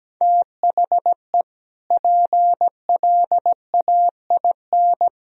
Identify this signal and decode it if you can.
Morse code, heard as a keyed tone, about 17 wpm, 700 Hz. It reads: THE PLAIN
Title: Sentences from Top 1000 Words - Review Part 2 - 17wpm